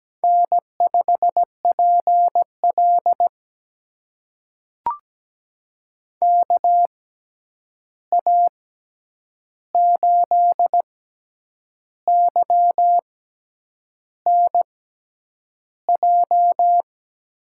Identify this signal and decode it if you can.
Morse code, heard as a keyed tone, about 17 wpm, spaced as for 8 wpm, 700 Hz. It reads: N5PL KA8YNJ
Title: USA Call Signs - 6x Character Spacing - 17wpm